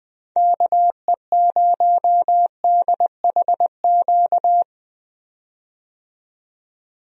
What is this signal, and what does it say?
Morse code, heard as a keyed tone, about 21 wpm, 700 Hz. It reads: KE0DHQ
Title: USA Call Signs - 6 Characters - 20wpm